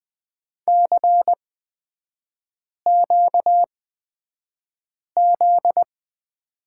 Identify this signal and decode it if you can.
Morse code, heard as a keyed tone, about 20 wpm, spaced as for 5 wpm, 700 Hz. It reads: CQZ
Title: 243 Lesson 39e - Letter z - Three Characters 20wpm